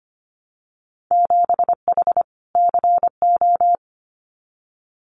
Morse code, heard as a keyed tone, about 25 wpm, 700 Hz. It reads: 7H CO
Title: Winter Field Day Exchanges Encore 25wpm